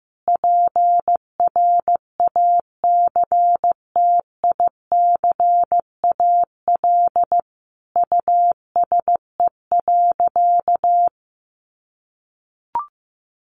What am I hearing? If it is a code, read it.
Morse code, heard as a keyed tone, about 15 wpm, 700 Hz. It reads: PRACTICAL USE.
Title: Sentences from Top 1400 Words 15wpm